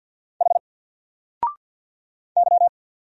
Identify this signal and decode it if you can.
Morse code, heard as a keyed tone, about 48 wpm, 700 Hz. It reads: H Y
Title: Single Letter-Number Alt 50wpm